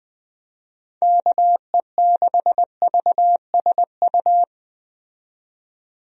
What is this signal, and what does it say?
Morse code, heard as a keyed tone, about 20 wpm, 700 Hz. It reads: KE6VSU
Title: USA Call Signs - 6 Characters - 20wpm